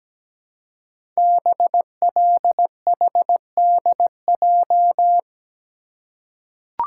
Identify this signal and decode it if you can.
Morse code, heard as a keyed tone, about 17 wpm, 700 Hz. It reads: BLHDJ
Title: Code Groups - 5 Letters - 17wpm